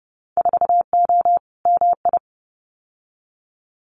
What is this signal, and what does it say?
Morse code, heard as a keyed tone, about 30 wpm, 700 Hz. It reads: 4O MI
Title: Winter Field Day Exchanges Encore 30wpm